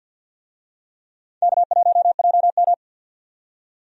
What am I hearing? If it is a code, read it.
Morse code, heard as a keyed tone, about 51 wpm, 700 Hz. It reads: K1JM
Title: USA Call Signs - 4 Characters - 50wpm